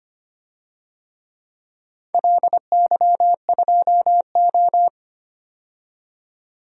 Morse code, heard as a keyed tone, about 25 wpm, 700 Hz. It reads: LY2O
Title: DXCC-100 Call Signs - 4 Characters - 25wpm